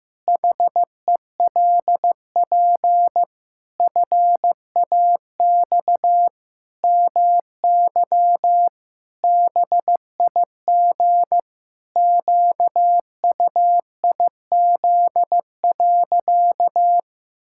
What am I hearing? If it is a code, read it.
Morse code, heard as a keyed tone, about 15 wpm, 700 Hz. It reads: HELP FAX MY BIG QUIZ.
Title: Pangram Challenge v2 - 15wpm